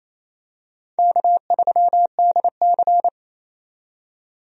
Morse code, heard as a keyed tone, about 28 wpm, 700 Hz. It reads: K3DC
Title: USA Call Signs - 4 Characters - 28wpm